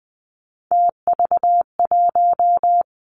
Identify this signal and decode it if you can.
Morse code, heard as a keyed tone, about 20 wpm, 700 Hz. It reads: TV1
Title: Three Challenging Letter-Number Combinations 20wpm